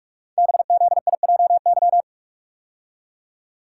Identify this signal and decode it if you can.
Morse code, heard as a keyed tone, about 44 wpm, 700 Hz. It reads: BZIJY
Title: Code Groups - 5 Letters - 45wpm